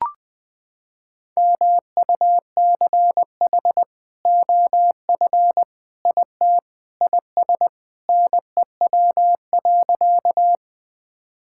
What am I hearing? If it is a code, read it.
Morse code, heard as a keyed tone, about 20 wpm, 700 Hz. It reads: MUCH OF IT IS NEW.
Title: Sentences with 2-Word Phrases - 20wpm